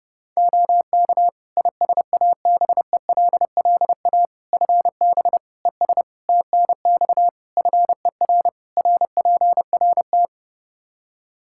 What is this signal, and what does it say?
Morse code, heard as a keyed tone, about 30 wpm, 700 Hz. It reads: OK ISABELLA FB ES TNX FER RPRT
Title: Practice QSO - Exchange 2 - Info - 30wpm